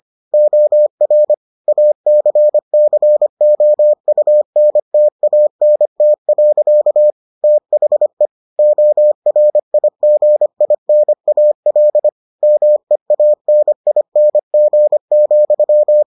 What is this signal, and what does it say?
Morse code, heard as a keyed tone, about 25 wpm, 600 Hz. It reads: OR ACCOUNTANT. THE ORIGINAL MEANING,